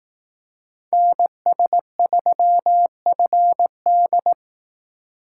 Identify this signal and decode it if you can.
Morse code, heard as a keyed tone, about 18 wpm, 700 Hz. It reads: NS3FD
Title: Speed Racing - USA 2x2 Call Signs 15wpm